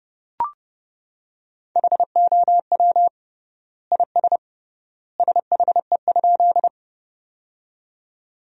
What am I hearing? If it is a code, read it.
Morse code, heard as a keyed tone, about 32 wpm, 700 Hz. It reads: HOW IS SHE?
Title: 3x Word-Spacing - Sentences from Top 100 Words - 30wpm